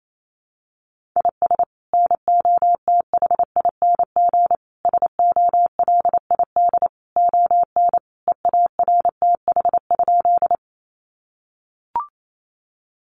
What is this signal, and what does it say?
Morse code, heard as a keyed tone, about 28 wpm, 700 Hz. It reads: IS NOTHING SOLID ON EARTH?
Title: Sentences from Top 1500 Words 28wpm